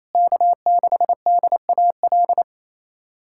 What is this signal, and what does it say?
Morse code, heard as a keyed tone, about 28 wpm, 700 Hz. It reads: K6DAL
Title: USA Callsign DE Callsign - 5 Characters - 28wpm